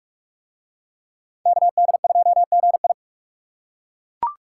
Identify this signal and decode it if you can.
Morse code, heard as a keyed tone, about 39 wpm, 700 Hz. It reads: KD2GI